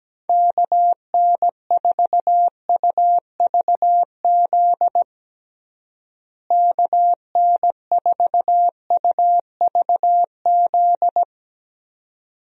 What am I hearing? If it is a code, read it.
Morse code, heard as a keyed tone, about 17 wpm, 700 Hz. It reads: KN4UVZ KN4UVZ
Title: USA Call Signs - 6 Characters - Repeated 2x - 17wpm